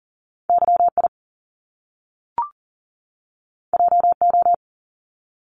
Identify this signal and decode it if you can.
Morse code, heard as a keyed tone, about 41 wpm, 700 Hz. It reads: YI JO
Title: Top Two Letter Combinations in Words 40wpm